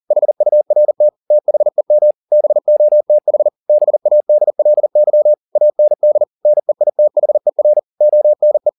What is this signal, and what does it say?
Morse code, heard as a keyed tone, about 40 wpm, 600 Hz. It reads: HURT THEM BOTH BADLY AND NEITHER ONE
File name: the old man and the sea 2 40wpm